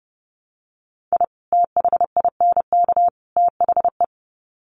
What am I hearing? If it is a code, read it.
Morse code, heard as a keyed tone, about 30 wpm, 700 Hz. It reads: I THINK THE
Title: Sets of 3 Words 30wpm